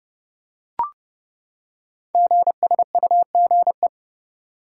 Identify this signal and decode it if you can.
Morse code, heard as a keyed tone, about 29 wpm, 700 Hz. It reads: GSUGE
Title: Code Groups - 5 Letters - 30wpm